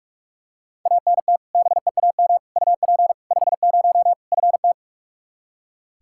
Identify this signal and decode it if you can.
Morse code, heard as a keyed tone, about 43 wpm, 700 Hz. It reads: ANT BEAM UP 50 FT
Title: K1USN SST QSO Practice 45wpm